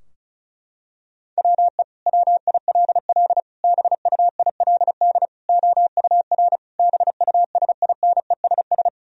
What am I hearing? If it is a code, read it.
Morse code, heard as a keyed tone, about 36 wpm, 700 Hz. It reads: WE WILL BUILD OUR BUSINESS
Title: Sentences from Top 1000 Words - Review Part 4 - 35wpm